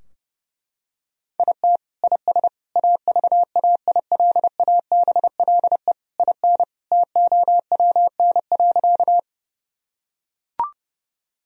Morse code, heard as a keyed tone, about 30 wpm, 700 Hz. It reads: IT IS AVAILABLE IN TOWN.
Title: Sentences from Top 1000 Words - Review Part 3 - 30wpm